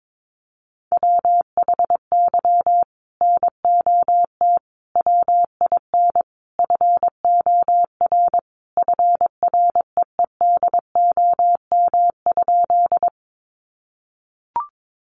Morse code, heard as a keyed tone, about 22 wpm, 700 Hz. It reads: WHY NOT WIN FOR FREEDOM?
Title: Sentences from Top 1500 Words 22wpm